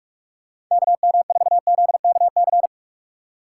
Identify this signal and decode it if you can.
Morse code, heard as a keyed tone, about 46 wpm, 700 Hz. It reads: KM4ZKC
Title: USA Call Signs - 6 Characters - 4x Character Spacing - 45wpm